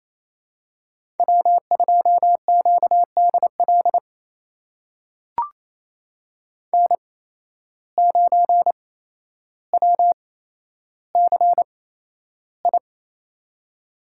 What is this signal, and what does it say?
Morse code, heard as a keyed tone, about 27 wpm, spaced as for 12 wpm, 700 Hz. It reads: W2QDL N9WCI
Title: USA Call Signs - 5 Characters - 8x Character Spacing - 28wpm